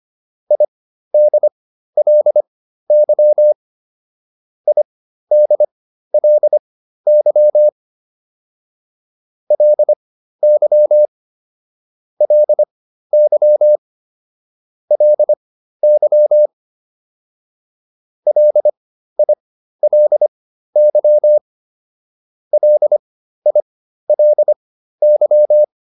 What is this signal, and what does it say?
Morse code, heard as a keyed tone, about 25 wpm, spaced as for 13 wpm, 600 Hz. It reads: IDLY IDLY LY LY LY LILY LILY